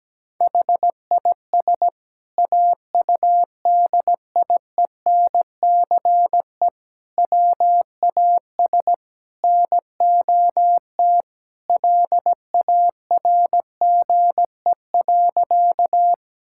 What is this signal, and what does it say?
Morse code, heard as a keyed tone, about 17 wpm, 700 Hz. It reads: HIS AUDIENCE WAS NOT LARGE.
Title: Sentences from Top 1000 Words - Review Part 5 - 17wpm